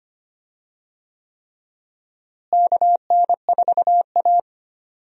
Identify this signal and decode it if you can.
Morse code, heard as a keyed tone, about 25 wpm, 700 Hz. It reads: KN4A
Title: USA Call Signs - 4 Characters - Repeated 3x - 25wpm